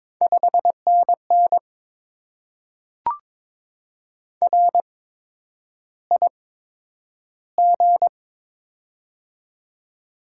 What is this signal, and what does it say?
Morse code, heard as a keyed tone, about 22 wpm, spaced as for 9 wpm, 700 Hz. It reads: 5NN RIG
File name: Top 20 QSO Elements - 8x Character Spacing - 22wpm